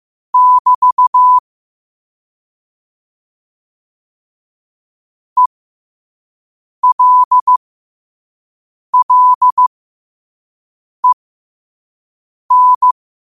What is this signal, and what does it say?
Morse code, heard as a keyed tone, about 15 wpm, spaced as for 5 wpm, 1000 Hz. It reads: <BT> ELLEN
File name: qotdfiles / 5wpm